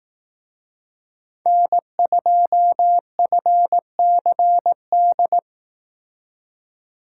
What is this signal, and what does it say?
Morse code, heard as a keyed tone, about 18 wpm, 700 Hz. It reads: N2FCD
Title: Speed Racing - USA 1x3 Call Signs 15wpm